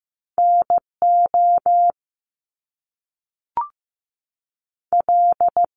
Morse code, heard as a keyed tone, about 15 wpm, 700 Hz. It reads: NO L
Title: Top Two Letter Combinations in Words 15wpm